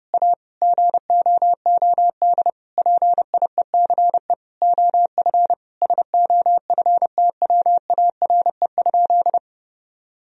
Morse code, heard as a keyed tone, about 32 wpm, 700 Hz. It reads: A GOOD PIECE OF SOFTWARE?
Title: Sentences from Top 1000 Words - Review Part 3 - 30wpm